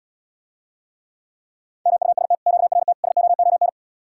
Morse code, heard as a keyed tone, about 77 wpm, 700 Hz. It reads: MUST QRT SOON